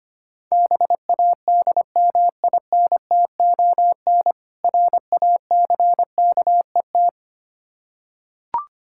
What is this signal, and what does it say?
Morse code, heard as a keyed tone, about 25 wpm, 700 Hz. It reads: BADMINTON RACKET
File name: practice / 25wpm / Difficult Words with Hints - 25wpm